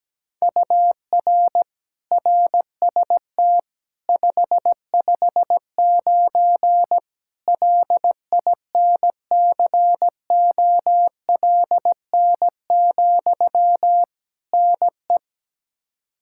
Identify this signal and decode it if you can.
Morse code, heard as a keyed tone, about 17 wpm, 700 Hz. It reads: UR RST 559 LINCOLN, NE
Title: Practice QSO - Exchange 1 - Beginning - 17wpm